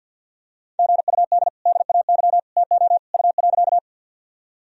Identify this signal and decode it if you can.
Morse code, heard as a keyed tone, about 51 wpm, 700 Hz. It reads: GUD DAY TO U.
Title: Sentences from Top 200 Words and Abbreviations 50wpm